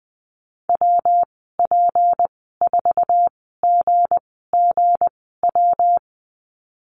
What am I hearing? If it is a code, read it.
Morse code, heard as a keyed tone, about 20 wpm, spaced as for 14 wpm, 700 Hz. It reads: WP4GGW
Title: 161 Lesson 26g - Letter g - Call Signs 20wpm